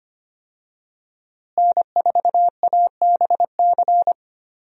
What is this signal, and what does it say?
Morse code, heard as a keyed tone, about 25 wpm, 700 Hz. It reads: N4ABC
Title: USA Call Signs - 5 Characters - Repeated 3x - 25wpm